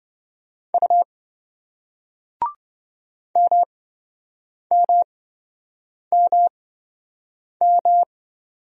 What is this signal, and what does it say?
Morse code, heard as a keyed tone, about 28 wpm, spaced as for 7 wpm, 700 Hz. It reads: U MMMM